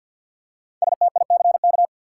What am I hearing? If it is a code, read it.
Morse code, heard as a keyed tone, about 52 wpm, 700 Hz. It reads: STICK
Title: Five Letter Words - 50wpm